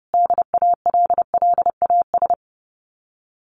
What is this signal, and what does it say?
Morse code, heard as a keyed tone, about 28 wpm, 700 Hz. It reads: DALLAS
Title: Common QSO Phrases Encore 30wpm